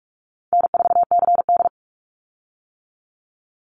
Morse code, heard as a keyed tone, about 46 wpm, 700 Hz. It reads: N4CD
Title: Easy Call Signs - 45wpm